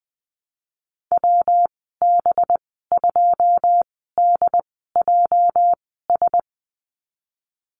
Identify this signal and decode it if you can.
Morse code, heard as a keyed tone, about 20 wpm, spaced as for 15 wpm, 700 Hz. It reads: WB2DJS
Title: 219 Lesson 35g - Number 0 - Call Signs 20wpm